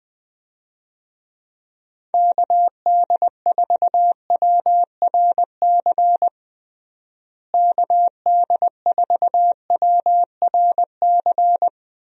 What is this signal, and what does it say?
Morse code, heard as a keyed tone, about 20 wpm, 700 Hz. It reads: KD4WRC KD4WRC
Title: USA Call Signs - 6 Characters - Repeated 3x - 20wpm